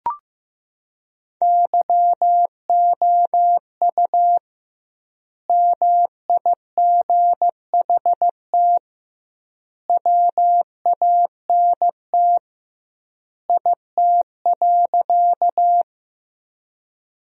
Morse code, heard as a keyed tone, about 15 wpm, 700 Hz. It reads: YOU MIGHT WANT IT.
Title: 2x Word-Spacing - Sentences from Top 300 Words - 15wpm